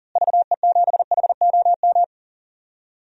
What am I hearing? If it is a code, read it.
Morse code, heard as a keyed tone, about 40 wpm, 700 Hz. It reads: VE7HOM
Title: DXCC-100 Call Signs - 6 Characters - Repeated 3x - 40wpm